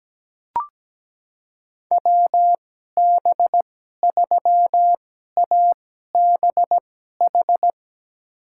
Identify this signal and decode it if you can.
Morse code, heard as a keyed tone, about 17 wpm, spaced as for 12 wpm, 700 Hz. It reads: WB3ABH
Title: USA Call Signs - 6 Characters - 2x Character Spacing - 17wpm